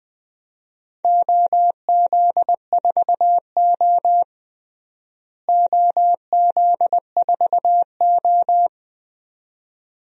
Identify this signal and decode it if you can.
Morse code, heard as a keyed tone, about 20 wpm, 700 Hz. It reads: OZ4O OZ4O
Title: DXCC-100 Call Signs - 4 Characters - Repeated 2x - 20wpm